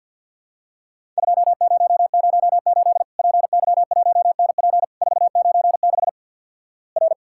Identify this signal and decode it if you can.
Morse code, heard as a keyed tone, about 50 wpm, 700 Hz. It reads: 2008 PY1NP 496 R
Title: Worked All Europe DX Contest - QTC Exchange - 50wpm